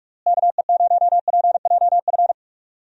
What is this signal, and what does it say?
Morse code, heard as a keyed tone, about 45 wpm, 700 Hz. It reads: KE0PJF
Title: USA Call Signs - 6 Characters - Repeated 3x - 45wpm